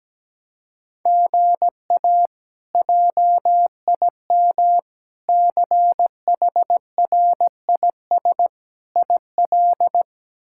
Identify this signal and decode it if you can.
Morse code, heard as a keyed tone, about 17 wpm, 700 Hz. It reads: GA JIM CHRIS IL
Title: K1USN SST QSO Practice - 17wpm